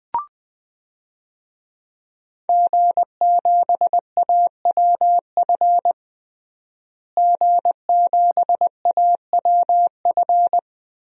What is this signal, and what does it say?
Morse code, heard as a keyed tone, about 20 wpm, 700 Hz. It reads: G7AWF G7AWF
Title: DXCC-100 Call Signs - 5 Characters - Repeated 2x - 20wpm